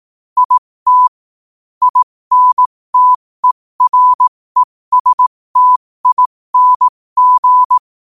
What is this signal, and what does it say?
Morse code, heard as a keyed tone, about 18 wpm, spaced as for 15 wpm, 1000 Hz. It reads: IT INTERESTING
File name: qotdfiles / 15wpm